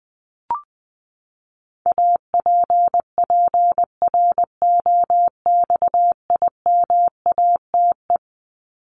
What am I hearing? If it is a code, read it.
Morse code, heard as a keyed tone, about 20 wpm, 700 Hz. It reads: APPROXIMATE
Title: LY Suffix - Top 100 Words with Base Words 20wpm